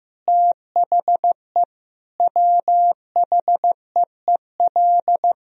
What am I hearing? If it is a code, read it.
Morse code, heard as a keyed tone, about 15 wpm, 700 Hz. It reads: THE WHEEL
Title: Sentences from Top 1000 Words - Review Part 3 - 15wpm